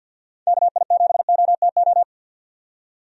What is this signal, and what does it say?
Morse code, heard as a keyed tone, about 52 wpm, 700 Hz. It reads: KI7OTO
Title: USA Call Signs - 6 Characters - 50wpm